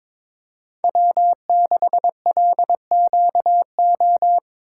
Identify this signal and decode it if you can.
Morse code, heard as a keyed tone, about 26 wpm, 700 Hz. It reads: W6LQO
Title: Speed Racing - USA Call Signs 22wpm